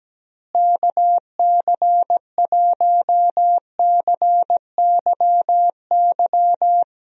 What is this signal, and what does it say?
Morse code, heard as a keyed tone, about 17 wpm, 700 Hz. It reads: KC1CYY
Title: USA Call Signs - 6 Characters - Repeated 2x - 17wpm